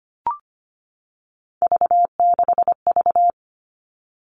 Morse code, heard as a keyed tone, about 25 wpm, 700 Hz. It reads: V6V
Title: All Cut Numbers - 3-Digit 25wpm